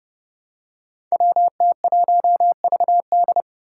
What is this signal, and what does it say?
Morse code, heard as a keyed tone, about 29 wpm, 700 Hz. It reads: WT1VD
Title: Five Challenging Letter-Number Combinations 30wpm